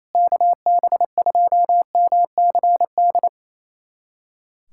Morse code, heard as a keyed tone, about 27 wpm, 700 Hz. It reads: KB2MCD
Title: USA Callsign DE Callsign 28wpm